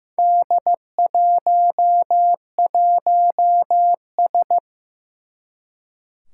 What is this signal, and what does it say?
Morse code, heard as a keyed tone, about 15 wpm, 700 Hz. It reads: D11S
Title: Four Challenging Letter-Number Combinations 15wpm